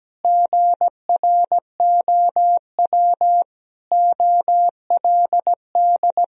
Morse code, heard as a keyed tone, about 17 wpm, 700 Hz. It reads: GROW OLD
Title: Sentences from Top 1000 Words - Review Part 1 - 17wpm